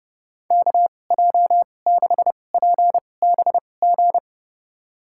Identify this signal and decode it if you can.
Morse code, heard as a keyed tone, about 30 wpm, spaced as for 22 wpm, 700 Hz. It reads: KJ6PBG
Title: USA Call Signs - 6 Characters - 2x Character Spacing - 30wpm